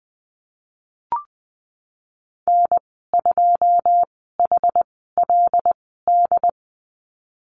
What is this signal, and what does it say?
Morse code, heard as a keyed tone, about 20 wpm, spaced as for 15 wpm, 700 Hz. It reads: N2HLD
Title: 147 Lesson 24g - Letter y - Call Signs 20wpm